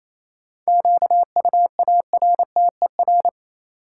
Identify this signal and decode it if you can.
Morse code, heard as a keyed tone, about 28 wpm, 700 Hz. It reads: QUARTER
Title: Sentences from Top 1300 Words - New Word Drill - 28wpm